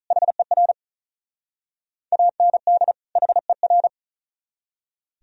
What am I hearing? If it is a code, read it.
Morse code, heard as a keyed tone, about 46 wpm, 700 Hz. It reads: HER AND HER